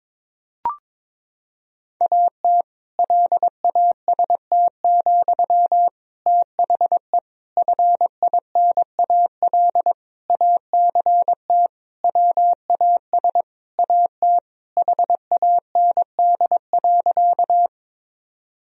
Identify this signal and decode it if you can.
Morse code, heard as a keyed tone, about 22 wpm, 700 Hz. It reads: AT LAST, THE FINAL ACT WAS AT HAND.
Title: Sentences from Top 1000 Words - Review Part 2 - 22wpm